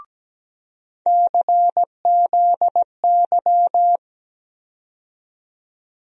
Morse code, heard as a keyed tone, about 17 wpm, 700 Hz. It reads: CZY
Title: USA Call Sign Postfixes - 3 Letters 17wpm